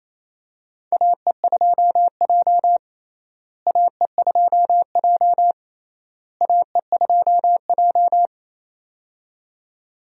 Morse code, heard as a keyed tone, about 28 wpm, 700 Hz. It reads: AE2J AE2J AE2J
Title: USA Call Signs - 4 Characters - Repeated 3x - 28wpm